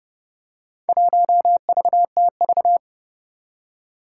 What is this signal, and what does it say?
Morse code, heard as a keyed tone, about 30 wpm, 700 Hz. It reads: 1VTV
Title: Four Challenging Letter-Number Combinations 30wpm